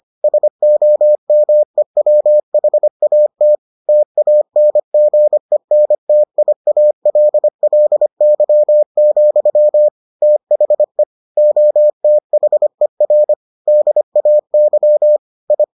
Morse code, heard as a keyed tone, about 25 wpm, 600 Hz. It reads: SOMEWHAT TANGENTIALLY, THE OTHER DAY I